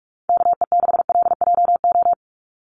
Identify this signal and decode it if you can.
Morse code, heard as a keyed tone, about 44 wpm, 700 Hz. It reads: KE6LJO
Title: USA Call Signs 45wpm